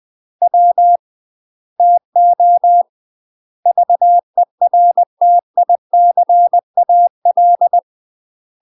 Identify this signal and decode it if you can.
Morse code, heard as a keyed tone, about 20 wpm, 700 Hz. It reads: W TO VERTICAL
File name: 20WPM